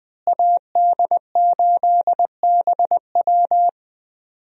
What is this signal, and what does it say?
Morse code, heard as a keyed tone, about 20 wpm, 700 Hz. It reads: AD8BW
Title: USA Call Signs - 5 Characters - Repeated 2x - 20wpm